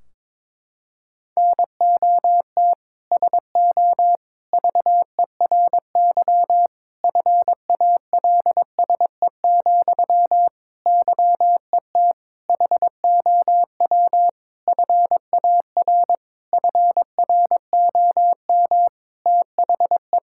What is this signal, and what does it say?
Morse code, heard as a keyed tone, about 22 wpm, 700 Hz. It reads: NOT SO VERY FALSE, YET HOW FAR FROM THE
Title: Sentences from Top 1600 Words 22wpm